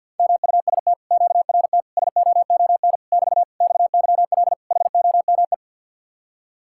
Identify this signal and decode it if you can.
Morse code, heard as a keyed tone, about 50 wpm, 700 Hz. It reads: MUST QRT SOON <BT> XYL HOME
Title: Speed-Racing Common QSO Phrases Encore - 50wpm